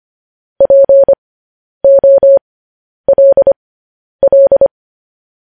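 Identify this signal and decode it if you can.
Morse code, heard as a keyed tone, about 25 wpm, spaced as for 10 wpm, 550 Hz. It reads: POLL
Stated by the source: Morse code news headlines practice transmission.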